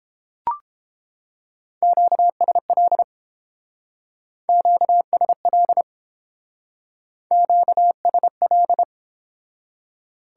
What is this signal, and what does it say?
Morse code, heard as a keyed tone, about 28 wpm, 700 Hz. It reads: QSL QSL QSL